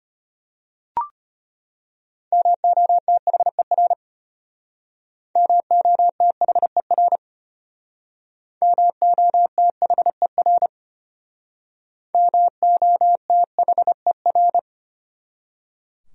Morse code, heard as a keyed tone, about 32 wpm, 700 Hz. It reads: MOTHER MOTHER MOTHER MOTHER